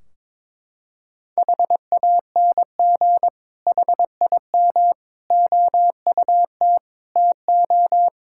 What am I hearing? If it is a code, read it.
Morse code, heard as a keyed tone, about 21 wpm, 700 Hz. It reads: HANG HIM OUT TO
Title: Sentences from Top 1600 Words 22wpm